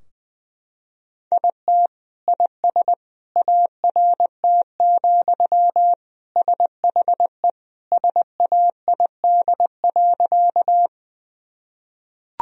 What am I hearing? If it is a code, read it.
Morse code, heard as a keyed tone, about 21 wpm, 700 Hz. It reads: IT IS ART, SHE SAID.
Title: Sentences from Top 1000 Words - Review Part 3 - 20wpm